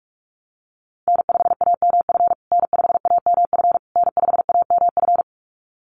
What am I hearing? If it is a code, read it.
Morse code, heard as a keyed tone, about 44 wpm, 700 Hz. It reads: N5AMF N5AMF N5AMF
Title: USA Call Signs Repeated 3x 45wpm